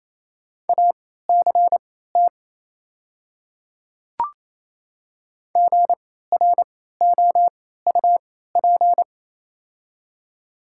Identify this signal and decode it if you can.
Morse code, heard as a keyed tone, about 28 wpm, spaced as for 16 wpm, 700 Hz. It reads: ACT GROUP
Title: 3x Character Spacing - Top 300 Words - 28wpm